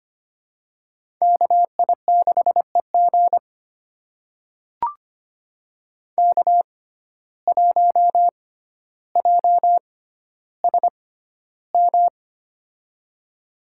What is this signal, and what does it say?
Morse code, heard as a keyed tone, about 25 wpm, spaced as for 13 wpm, 700 Hz. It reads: KI6EG K1JSM
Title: USA Call Signs - 5 Characters - 6x Character Spacing - 25wpm